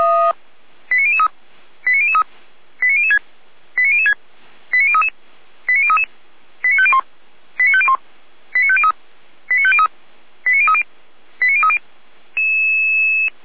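Five-tone selective call (ZVEI), heard as quick runs of five tones, tones of about 70 ms. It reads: 89003 89003 89006 89006 89033 89033 89501 89501 89503 89503 89030 89030
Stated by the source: ZVEI five-tone selective call received on an SDR, recorded on the Feuerwehr Orlamünde fire brigade server.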